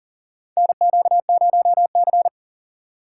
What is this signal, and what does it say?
Morse code, heard as a keyed tone, about 40 wpm, 700 Hz. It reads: NQ0C
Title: USA Call Signs - 4 Characters - 2x Character Spacing - 40wpm